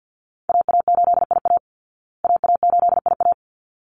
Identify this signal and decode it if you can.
Morse code, heard as a keyed tone, about 49 wpm, 700 Hz. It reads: AA8IA AA8IA
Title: Call Signs Repeated 3x - 50wpm